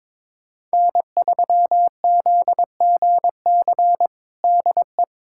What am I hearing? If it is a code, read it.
Morse code, heard as a keyed tone, about 22 wpm, 700 Hz. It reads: N3ZGC DE
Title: USA Callsign DE Callsign - 5 Characters - 22wpm